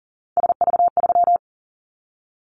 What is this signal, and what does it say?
Morse code, heard as a keyed tone, about 38 wpm, 700 Hz. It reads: SV3
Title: Callsign Prefixes for Top 50 DXCC - 40wpm